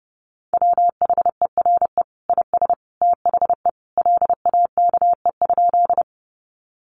Morse code, heard as a keyed tone, about 30 wpm, 700 Hz. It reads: WHERE IS THE LAKE?
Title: Sentences from Top 1100 Words 30wpm